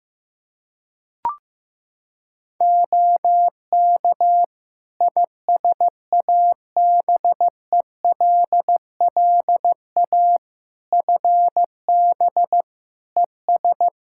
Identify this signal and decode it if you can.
Morse code, heard as a keyed tone, about 15 wpm, 700 Hz. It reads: OK ISABELLA FB ES
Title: Practice QSO - Exchange 2 - Info - 15wpm